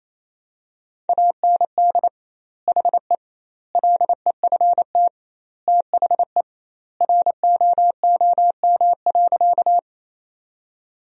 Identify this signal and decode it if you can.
Morse code, heard as a keyed tone, about 28 wpm, 700 Hz. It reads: AND HE LEFT THE ROOM.
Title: 2x Word-Spacing - Sentences from Top 300 Words - 28wpm